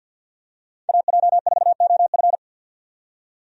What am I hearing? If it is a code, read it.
Morse code, heard as a keyed tone, about 46 wpm, 700 Hz. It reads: AJ4OF
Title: USA Call Signs - 5 Characters - 50wpm